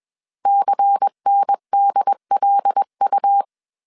Morse code, heard as a keyed tone, about 21 wpm, 800 Hz. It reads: CNDLU
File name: CW / DET / 1 6 1500C S A 20wpm 2048